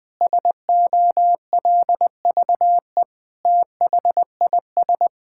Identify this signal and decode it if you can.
Morse code, heard as a keyed tone, about 20 wpm, 700 Hz. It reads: SOLVE THIS
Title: Sentences with 7-Word Phrases - 20wpm